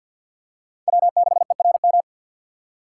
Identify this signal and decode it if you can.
Morse code, heard as a keyed tone, about 47 wpm, 700 Hz. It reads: W6ERM
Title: USA Call Signs - 5 Characters - 6x Character Spacing - 50wpm